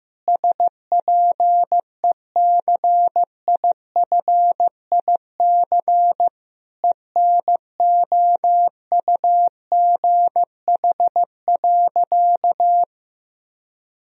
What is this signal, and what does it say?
Morse code, heard as a keyed tone, about 15 wpm, 700 Hz. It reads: SPECIFIC ENOUGH.
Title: Sentences from Top 1000 Words - Review Part 3 - 15wpm